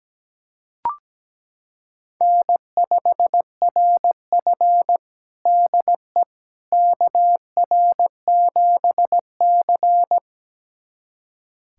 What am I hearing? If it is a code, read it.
Morse code, heard as a keyed tone, about 17 wpm, 700 Hz. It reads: N5RF DE KR7C
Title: USA Callsign DE Callsign - 4 Characters - 17wpm